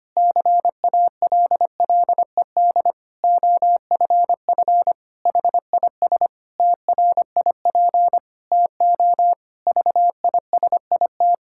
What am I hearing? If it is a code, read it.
Morse code, heard as a keyed tone, about 25 wpm, 700 Hz. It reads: CALLED OFF HIS TRIP TO VISIT
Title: Pangram Challenge v2 - 25wpm